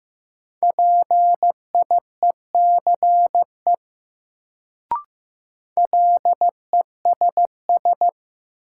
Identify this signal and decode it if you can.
Morse code, heard as a keyed tone, about 15 wpm, 700 Hz. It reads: PIECE LESS
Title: Sentences from Top 400 Words - New Word Drill - 15wpm